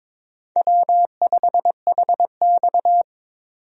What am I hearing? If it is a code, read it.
Morse code, heard as a keyed tone, about 22 wpm, 700 Hz. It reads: W5HX
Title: USA Call Signs - 4 Characters - 22wpm